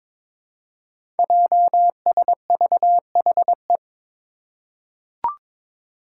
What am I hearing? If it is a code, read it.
Morse code, heard as a keyed tone, about 22 wpm, 700 Hz. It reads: JSVHE
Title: Five Challenging Letter-Number Combinations 22wpm